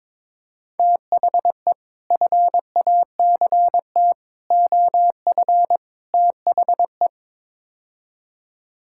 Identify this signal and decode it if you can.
Morse code, heard as a keyed tone, about 22 wpm, 700 Hz. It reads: THE FACT OF THE
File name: practice / 22wpm / Build A Phrase - 5-Word Phrases - 22wpm